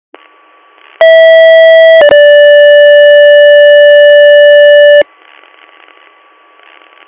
Two-tone page, heard as a one-second tone, then a three-second tone, about 650 then 580 Hz, with noise. It is heard from 1.0 to 5.0 s.